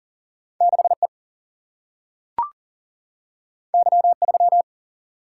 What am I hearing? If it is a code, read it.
Morse code, heard as a keyed tone, about 41 wpm, 700 Hz. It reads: 6E Y3
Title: Single-Letter Single-Number 40wpm